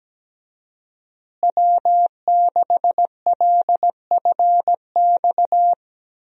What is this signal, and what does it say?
Morse code, heard as a keyed tone, about 17 wpm, 700 Hz. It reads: W6LFX
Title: USA Call Signs - 5 Characters - Repeated 2x - 17wpm